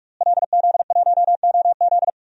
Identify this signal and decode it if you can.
Morse code, heard as a keyed tone, about 46 wpm, 700 Hz. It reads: LZ1OZ
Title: DXCC-100 Call Signs - 5 Characters - Repeated 2x - 45wpm